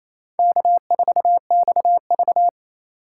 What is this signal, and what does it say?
Morse code, heard as a keyed tone, about 34 wpm, 700 Hz. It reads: K4XV
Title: Speed Racing - USA 1x2 Call Signs 28wpm